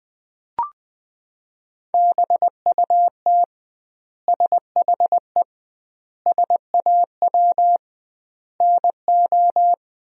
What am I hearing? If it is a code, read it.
Morse code, heard as a keyed tone, about 19 wpm, 700 Hz. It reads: BUT SHE SAW NO
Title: 2x Word-Spacing - Sentences from Top 300 Words - 20wpm